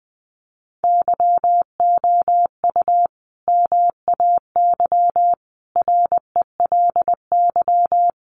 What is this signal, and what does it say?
Morse code, heard as a keyed tone, about 20 wpm, 700 Hz. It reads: YOU MAY RELY
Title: Sentences from Top 1200 Words 20wpm